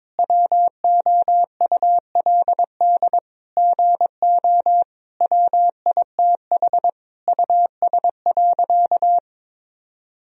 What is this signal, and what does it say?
Morse code, heard as a keyed tone, about 22 wpm, 700 Hz. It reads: WOULD GO WITH US.
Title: Sentences from Top 1000 Words - Review Part 3 - 22wpm